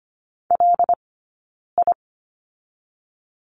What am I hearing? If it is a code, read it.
Morse code, heard as a keyed tone, about 25 wpm, 700 Hz. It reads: L I
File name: Lesson 15d - Number 5 - Two Characters 25wpm